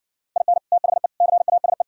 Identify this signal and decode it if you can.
Morse code, heard as a keyed tone, about 60 wpm, 700 Hz. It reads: IN THE CASE